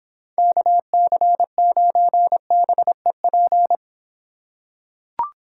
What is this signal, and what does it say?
Morse code, heard as a keyed tone, about 26 wpm, 700 Hz. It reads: KC9BEP